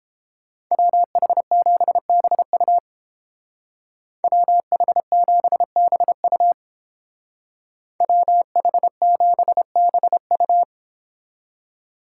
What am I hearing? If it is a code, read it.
Morse code, heard as a keyed tone, about 30 wpm, 700 Hz. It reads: WH7BU WH7BU WH7BU